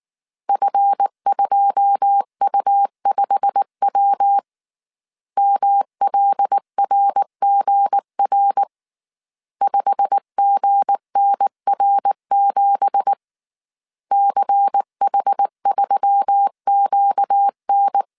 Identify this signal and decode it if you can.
Morse code, heard as a keyed tone, about 19 wpm, 800 Hz. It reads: F2U5W MLRGR 5GNR7 CH3QN
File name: CW / DET / 1 9 1500C S A 18wpm 2459